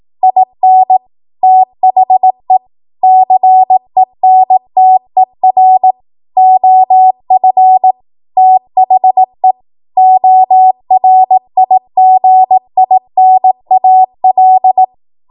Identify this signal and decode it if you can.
Morse code, heard as a keyed tone, about 18 wpm, 750 Hz. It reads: IN THE CENTER OF THE ORIGINAL
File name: Archive / 18 WPM / 18WPM